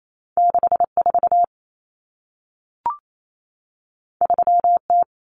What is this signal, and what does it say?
Morse code, heard as a keyed tone, about 28 wpm, 700 Hz. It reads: 64 3T
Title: Common Cut Numbers - 2-Digit 28wpm